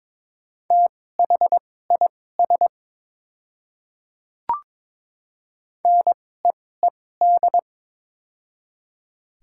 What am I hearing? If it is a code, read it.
Morse code, heard as a keyed tone, about 22 wpm, spaced as for 16 wpm, 700 Hz. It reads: THIS NEED
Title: 2x Character Spacing - Top 200 Words - 22wpm